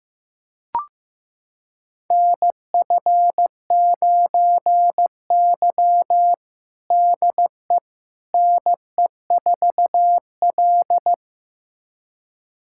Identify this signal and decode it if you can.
Morse code, heard as a keyed tone, about 15 wpm, 700 Hz. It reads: NF9Y DE NE4L
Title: USA Callsign DE Callsign - 4 Characters - 15wpm